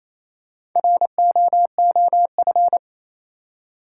Morse code, heard as a keyed tone, about 28 wpm, 700 Hz. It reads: ROOF
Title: Four Letter Words - 28wpm